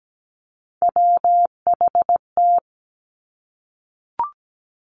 Morse code, heard as a keyed tone, about 17 wpm, 700 Hz. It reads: WHT
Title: Three Challenging Letter-Number Combinations 17wpm